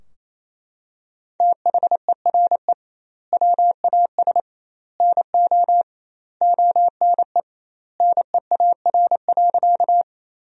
Morse code, heard as a keyed tone, about 27 wpm, 700 Hz. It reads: THERE WAS NO ONE NEAR.
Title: 2x Word-Spacing - Sentences from Top 200 Words - 28wpm